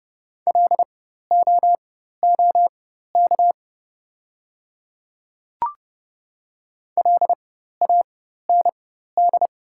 Morse code, heard as a keyed tone, about 31 wpm, spaced as for 14 wpm, 690 Hz. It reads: LOOK LAND
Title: 4x Character Spacing - Top 200 Words - 30wpm